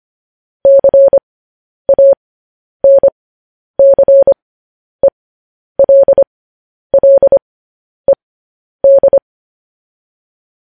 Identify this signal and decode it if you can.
Morse code, heard as a keyed tone, about 25 wpm, spaced as for 10 wpm, 550 Hz. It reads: CANCELLED
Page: News Headlines in Morse Code at 10 WPM